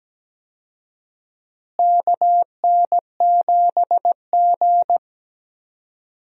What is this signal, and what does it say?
Morse code, heard as a keyed tone, about 17 wpm, 700 Hz. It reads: KN7G
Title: USA Call Signs - 4 Characters - Repeated 2x - 17wpm